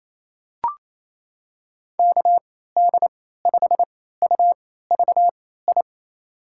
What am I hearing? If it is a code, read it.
Morse code, heard as a keyed tone, about 28 wpm, spaced as for 16 wpm, 700 Hz. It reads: KD5UVI
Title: USA Call Signs - 6 Characters - 3x Character Spacing - 28wpm